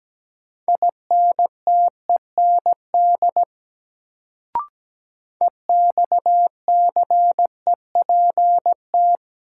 Morse code, heard as a keyed tone, about 17 wpm, 700 Hz. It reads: INTEND EXCEPT
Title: Sentences from Top 1600 Words - New Word Drill Alt - 17wpm